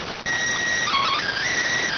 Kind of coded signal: SSTV (slow-scan television)

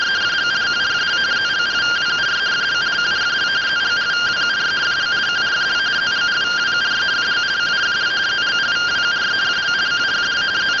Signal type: two-tone teleprinter signal

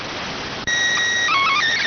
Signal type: SSTV (slow-scan television)

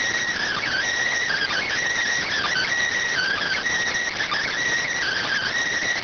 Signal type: Emergency Alert System (SAME) tones

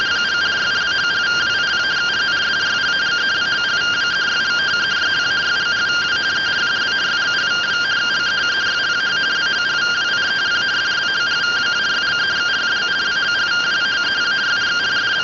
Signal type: two-tone teleprinter signal

